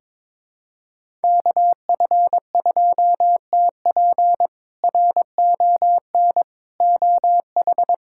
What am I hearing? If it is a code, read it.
Morse code, heard as a keyed tone, about 22 wpm, 700 Hz. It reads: KF2TP RON OH
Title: K1USN SST QSO Practice - 22wpm